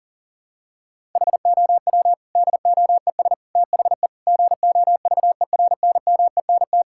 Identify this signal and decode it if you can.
Morse code, heard as a keyed tone, about 40 wpm, 700 Hz. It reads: HOW DOES THE GOVERNMENT
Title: Sentences from Top 1000 Words - Review Part 5 - 40wpm